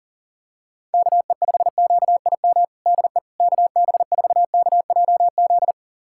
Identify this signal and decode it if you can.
Morse code, heard as a keyed tone, about 40 wpm, 700 Hz. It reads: KE5QIM DE KB4KJZ
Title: USA Callsign DE Callsign - 6 Characters - 40wpm